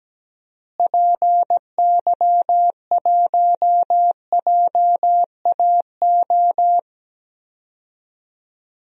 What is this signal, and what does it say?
Morse code, heard as a keyed tone, about 17 wpm, 700 Hz. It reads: PY1JAO
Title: DXCC-100 Call Signs - 6 Characters - Repeated 2x - 17wpm